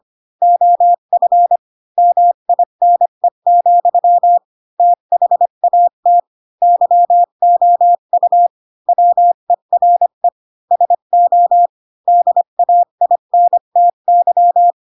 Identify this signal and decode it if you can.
Morse code, heard as a keyed tone, about 25 wpm, 700 Hz. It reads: OF MINE, THAT YOU WERE SO DAINTY
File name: Pinocchio / Pinocchio ch7 20 25 ik1hge